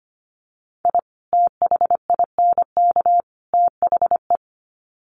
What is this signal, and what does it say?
Morse code, heard as a keyed tone, about 25 wpm, 700 Hz. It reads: I THINK THE
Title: Sets of 3 Words 25wpm